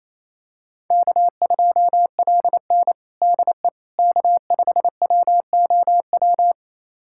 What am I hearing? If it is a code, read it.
Morse code, heard as a keyed tone, about 28 wpm, 700 Hz. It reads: K2LN DE K5WOW
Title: USA Callsign DE Callsign 28wpm